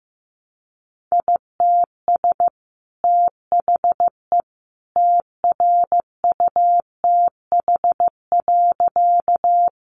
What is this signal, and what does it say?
Morse code, heard as a keyed tone, about 15 wpm, 700 Hz. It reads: ITS THE TRUTH.
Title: Sentences from Top 800 Words 15wpm